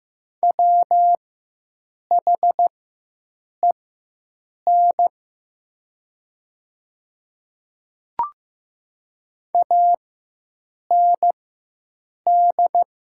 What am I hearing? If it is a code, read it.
Morse code, heard as a keyed tone, about 15 wpm, spaced as for 7 wpm, 700 Hz. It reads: WHEN AND
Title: 4x Character Spacing - Top 100 Words - 15wpm